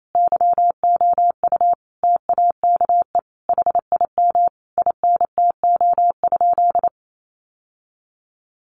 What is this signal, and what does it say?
Morse code, heard as a keyed tone, about 28 wpm, 700 Hz. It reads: YOU TAKE HIM INTO?
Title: Sentences from Top 600 Words 28wpm